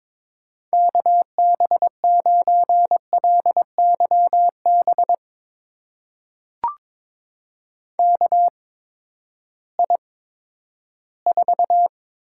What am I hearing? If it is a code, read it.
Morse code, heard as a keyed tone, about 22 wpm, spaced as for 13 wpm, 700 Hz. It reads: KB9LYB KI4
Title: USA Call Signs - 6 Characters - 8x Character Spacing - 22wpm